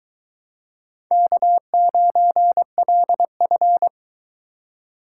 Morse code, heard as a keyed tone, about 23 wpm, 700 Hz. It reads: K9LF